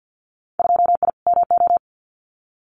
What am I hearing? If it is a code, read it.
Morse code, heard as a keyed tone, about 50 wpm, 700 Hz. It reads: 2I MO
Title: Winter Field Day Exchanges 50wpm